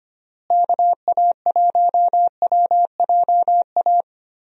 Morse code, heard as a keyed tone, about 25 wpm, 700 Hz. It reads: KA1WJA
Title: USA Call Signs - 6 Characters - Repeated 3x - 25wpm